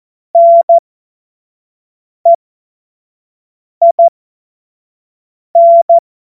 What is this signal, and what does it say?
Morse code, heard as a keyed tone, about 14 wpm, spaced as for 5 wpm, 680 Hz. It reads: NEIN
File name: Morsekurs / Witze / Witz 22. Juni 2021, 25 F70